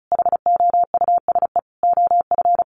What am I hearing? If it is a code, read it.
Morse code, heard as a keyed tone, about 34 wpm, 700 Hz. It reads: HOUSE OF
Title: Sets of 5 Words Encore 35wpm